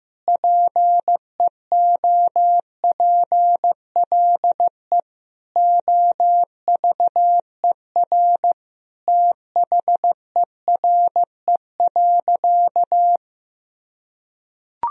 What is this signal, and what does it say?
Morse code, heard as a keyed tone, about 16 wpm, 700 Hz. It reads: PEOPLE OVER THERE.
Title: Sentences from Top 1000 Words - Review Part 1 - 15wpm